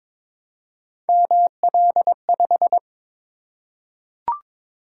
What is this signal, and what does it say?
Morse code, heard as a keyed tone, about 22 wpm, 700 Hz. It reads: ML5
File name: practice / 22wpm / Three Letter-Number Combinations - 22wpm